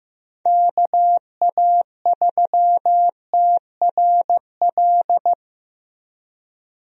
Morse code, heard as a keyed tone, about 15 wpm, 700 Hz. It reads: KA3TRL
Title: USA Call Signs - 6 Characters - Repeated 3x - 15wpm